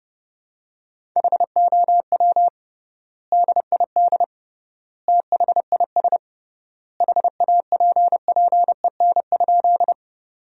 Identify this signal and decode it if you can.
Morse code, heard as a keyed tone, about 29 wpm, 700 Hz. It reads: HOW DID THIS HAPPEN?
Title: 3x Word-Spacing - Sentences from Top 400 Words - 30wpm